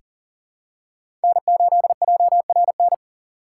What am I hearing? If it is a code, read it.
Morse code, heard as a keyed tone, about 40 wpm, 700 Hz. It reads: N8JRN
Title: USA Call Signs - 5 Characters - 4x Character Spacing - 40wpm